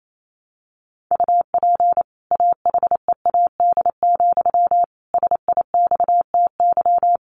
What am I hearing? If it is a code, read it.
Morse code, heard as a keyed tone, about 28 wpm, 700 Hz. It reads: UP AHEAD, SIXTY
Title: A Sound Of Thunder 28wpm